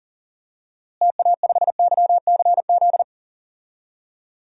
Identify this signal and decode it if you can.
Morse code, heard as a keyed tone, about 40 wpm, 700 Hz. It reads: TA5YCZ
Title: DXCC-100 Call Signs - 6 Characters - Repeated 3x - 40wpm